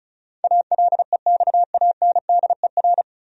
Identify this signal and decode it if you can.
Morse code, heard as a keyed tone, about 36 wpm, 700 Hz. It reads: ALEXANDER
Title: USA Common Names - 35wpm